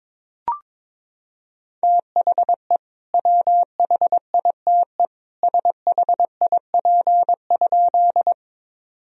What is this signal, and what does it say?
Morse code, heard as a keyed tone, about 21 wpm, 700 Hz. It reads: THE WHITE SHIP?
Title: Sentences from Top 1000 Words - Review Part 2 - 22wpm